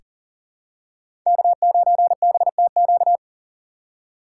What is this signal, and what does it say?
Morse code, heard as a keyed tone, about 40 wpm, 700 Hz. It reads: K9BTQ
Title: K1USN SST QSO Practice - 40wpm